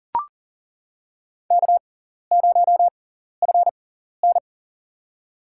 Morse code, heard as a keyed tone, about 40 wpm, spaced as for 14 wpm, 700 Hz. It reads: K0FN
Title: USA Call Signs - 4 Characters - 6x Character Spacing - 40wpm